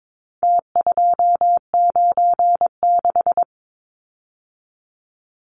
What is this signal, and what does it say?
Morse code, heard as a keyed tone, about 22 wpm, 700 Hz. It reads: T296
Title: Numbers - Contest Style - 4-Digit 22wpm